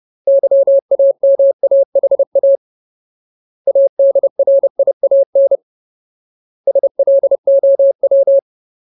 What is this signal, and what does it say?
Morse code, heard as a keyed tone, about 30 wpm, 550 Hz. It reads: YAMAHA ADRIAN SLOW
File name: Ultimate Word List / 30WPM Ultimate